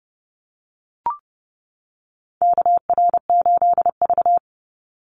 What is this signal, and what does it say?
Morse code, heard as a keyed tone, about 30 wpm, 700 Hz. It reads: KR8V
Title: Call Signs - 30wpm